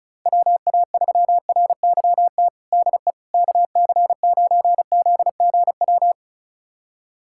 Morse code, heard as a keyed tone, about 36 wpm, 700 Hz. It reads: WA3RYT DE KC9ZGW
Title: USA Callsign DE Callsign 35wpm